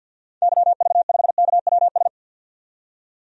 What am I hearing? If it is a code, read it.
Morse code, heard as a keyed tone, about 50 wpm, 700 Hz. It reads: YV5KWS
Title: DXCC-100 Call Signs - 6 Characters - 50wpm